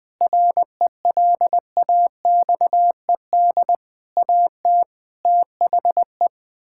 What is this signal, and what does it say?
Morse code, heard as a keyed tone, about 21 wpm, 700 Hz. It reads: RELAXED AT THE
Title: Sentences with 6-Word Phrases - 20wpm